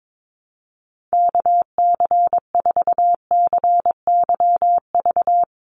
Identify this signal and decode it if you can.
Morse code, heard as a keyed tone, about 22 wpm, 700 Hz. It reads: KC4CYV
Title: USA Call Signs 22wpm